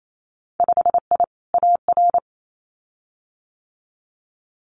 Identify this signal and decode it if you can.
Morse code, heard as a keyed tone, about 28 wpm, 700 Hz. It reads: 5I AR
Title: Winter Field Day Exchanges Encore 28wpm